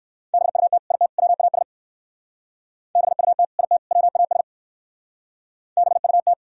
Morse code, heard as a keyed tone, about 64 wpm, 700 Hz. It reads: BUT IT WAS BUT IT WAS BUT